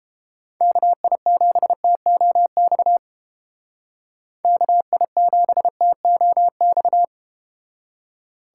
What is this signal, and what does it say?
Morse code, heard as a keyed tone, about 32 wpm, 700 Hz. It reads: KI7TOX KI7TOX